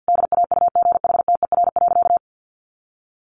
Morse code, heard as a keyed tone, about 50 wpm, 700 Hz. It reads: DAUGHTER.
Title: Sentences from Top 1500 Words 50wpm